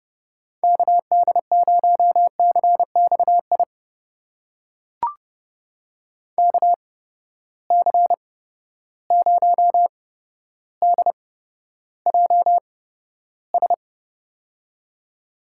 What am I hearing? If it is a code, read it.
Morse code, heard as a keyed tone, about 30 wpm, spaced as for 13 wpm, 700 Hz. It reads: KD0CXI KC0DJS
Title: USA Call Signs - 6 Characters - 8x Character Spacing - 30wpm